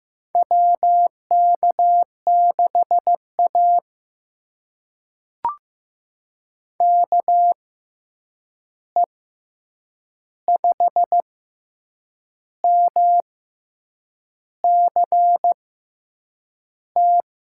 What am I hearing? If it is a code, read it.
Morse code, heard as a keyed tone, about 15 wpm, spaced as for 7 wpm, 700 Hz. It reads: WK6A KE5MCT
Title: USA Call Signs - 6x Character Spacing - 15wpm